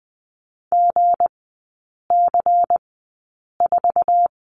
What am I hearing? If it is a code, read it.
Morse code, heard as a keyed tone, about 20 wpm, spaced as for 8 wpm, 700 Hz. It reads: GC4
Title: 166 Lesson 27e - Number 7 - Three Characters 20wpm